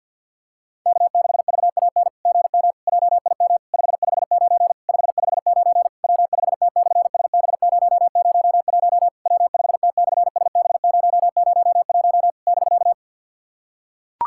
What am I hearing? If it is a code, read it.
Morse code, heard as a keyed tone, about 46 wpm, 690 Hz. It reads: K6VAN GM JIM 559 559 W5T/SB001 W5T/SB001 <BK>
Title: SOTA QSO Practice 50wpm